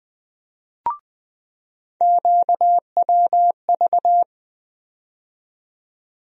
Morse code, heard as a keyed tone, about 20 wpm, 700 Hz. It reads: QWV
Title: USA Call Sign Postfixes - 3 Letters 20wpm